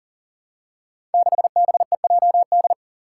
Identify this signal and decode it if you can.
Morse code, heard as a keyed tone, about 42 wpm, 700 Hz. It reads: 6BEJD
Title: Five Challenging Letter-Number Combinations 40wpm